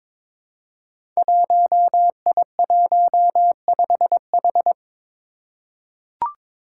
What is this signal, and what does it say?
Morse code, heard as a keyed tone, about 22 wpm, 700 Hz. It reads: 1I15H
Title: Five Challenging Letter-Number Combinations 22wpm